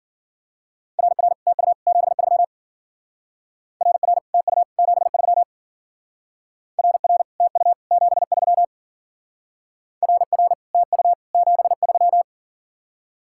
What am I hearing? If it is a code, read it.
Morse code, heard as a keyed tone, about 51 wpm, 700 Hz. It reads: RR TU 73 RR TU 73 RR TU 73 RR TU 73